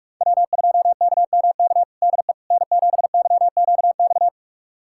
Morse code, heard as a keyed tone, about 46 wpm, 700 Hz. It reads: W2KMK DE N7YQX
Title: USA Callsign DE Callsign - 5 Characters - 45wpm